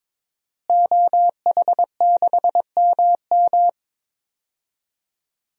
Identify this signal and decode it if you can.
Morse code, heard as a keyed tone, about 22 wpm, 700 Hz. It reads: OH6MM
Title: DXCC-100 Call Signs - 5 Characters - Repeated 3x - 22wpm